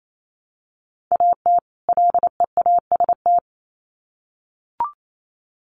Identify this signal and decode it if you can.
Morse code, heard as a keyed tone, about 28 wpm, 700 Hz. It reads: AT LEAST
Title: Sets of 2 Words 28wpm